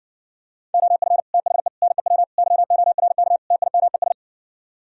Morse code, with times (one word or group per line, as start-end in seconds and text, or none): 0.7-1.2 OF
1.3-1.7 THE
1.8-2.2 NEW
2.4-3.4 YORK
3.5-4.1 TIMES